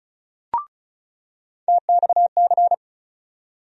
Morse code, heard as a keyed tone, about 36 wpm, 700 Hz. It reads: TXC
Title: USA Call Sign Postfixes - 3 Letters 35wpm